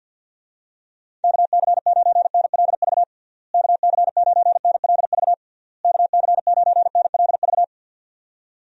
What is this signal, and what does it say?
Morse code, heard as a keyed tone, about 52 wpm, 700 Hz. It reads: KC9NLV KC9NLV KC9NLV
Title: USA Call Signs - 6 Characters - Repeated 3x - 50wpm